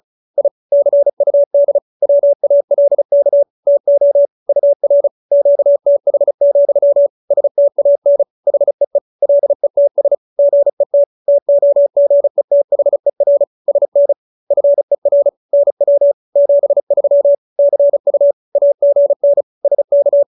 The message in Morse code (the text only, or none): I CUD WALK TO UR QTH, STAN HEE LETS GET TOGETHER SN FER NW 73 CU AGN SK